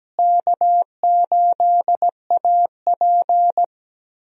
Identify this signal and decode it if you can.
Morse code, heard as a keyed tone, about 17 wpm, 700 Hz. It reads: K8AP
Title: USA Call Signs - 4 Characters - Repeated 2x - 17wpm